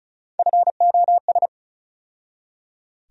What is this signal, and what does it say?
Morse code, heard as a keyed tone, about 36 wpm, 700 Hz. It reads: FOS
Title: USA Call Sign Postfixes - 3 Letters 35wpm